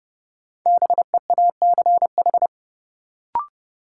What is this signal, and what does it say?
Morse code, heard as a keyed tone, about 30 wpm, 700 Hz. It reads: BEACH
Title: Sentences from Top 1500 Words - New Word Drill Alt - 30wpm